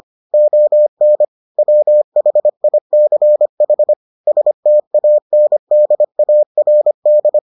Morse code, heard as a keyed tone, about 25 wpm, 600 Hz. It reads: ON WHICH STANDARD